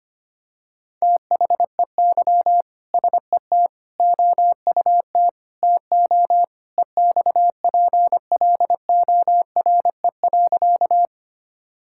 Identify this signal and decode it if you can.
Morse code, heard as a keyed tone, about 25 wpm, 700 Hz. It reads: THEY SET OUT TO EXPLORE.
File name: practice / 25wpm / Sentences with 2-Word Phrases - 25wpm